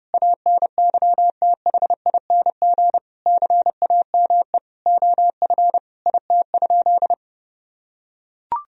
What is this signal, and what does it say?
Morse code, heard as a keyed tone, about 30 wpm, 700 Hz. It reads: ANYTHING CAME OF IT?
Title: Sentences from Top 1000 Words - Review Part 3 - 30wpm